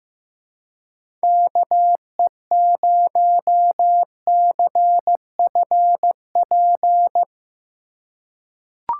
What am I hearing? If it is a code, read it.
Morse code, heard as a keyed tone, about 15 wpm, 700 Hz. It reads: KE0CFP
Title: USA Call Signs - 4x Character Spacing - 15wpm